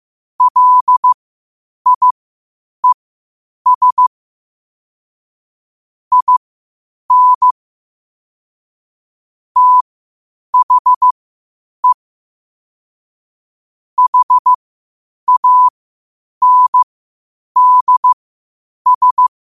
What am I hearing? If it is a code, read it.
Morse code, heard as a keyed tone, about 15 wpm, spaced as for 8 wpm, 1000 Hz. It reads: LIES IN THE HANDS
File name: qotdfiles / 8wpm